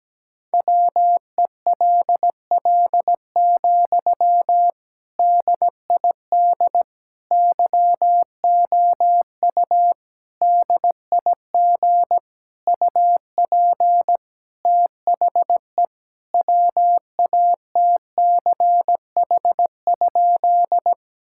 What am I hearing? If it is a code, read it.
Morse code, heard as a keyed tone, about 17 wpm, 700 Hz. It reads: WELL, DID YOU DIG UP THE WATCH?
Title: Sentences from Top 1600 Words 17wpm